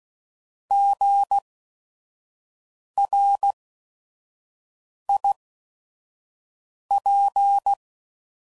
Morse code, heard as a keyed tone, about 16 wpm, spaced as for 5 wpm, 800 Hz. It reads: GRIP
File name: Prefixes / By First Letter / Gs